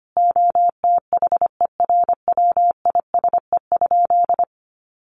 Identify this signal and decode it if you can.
Morse code, heard as a keyed tone, about 25 wpm, 700 Hz. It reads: OTHERWISE?
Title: Sentences from Top 1200 Words 25wpm